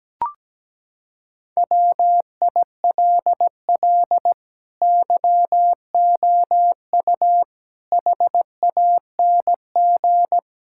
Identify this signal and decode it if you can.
Morse code, heard as a keyed tone, about 17 wpm, 700 Hz. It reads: WILL YOU HANG
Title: Sentences from Top 1600 Words 17wpm